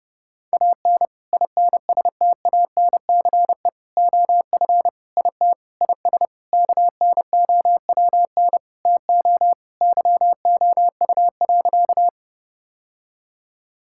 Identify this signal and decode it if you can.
Morse code, heard as a keyed tone, about 30 wpm, 700 Hz. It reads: AN INSTANCE OF IT IS KNOWN TO YOU.
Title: Sentences from Top 1000 Words - Review Part 5 - 30wpm